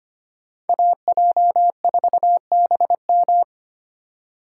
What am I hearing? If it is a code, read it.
Morse code, heard as a keyed tone, about 25 wpm, 700 Hz. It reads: AJ4BM
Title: USA Call Signs - 5 Characters - 2x Character Spacing - 25wpm